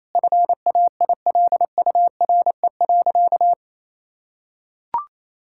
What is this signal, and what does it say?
Morse code, heard as a keyed tone, about 28 wpm, 700 Hz. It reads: FAILURE.
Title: Sentences from Top 1000 Words - Review Part 6 - 28wpm